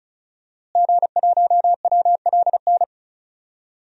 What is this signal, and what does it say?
Morse code, heard as a keyed tone, about 34 wpm, 700 Hz. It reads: G1WLN
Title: DXCC-100 Call Signs - 5 Characters - Repeated 3x - 35wpm